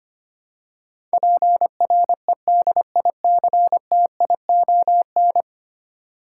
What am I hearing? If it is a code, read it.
Morse code, heard as a keyed tone, about 25 wpm, 700 Hz. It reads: PREDICTION
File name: Ten Letter Words - 25wpm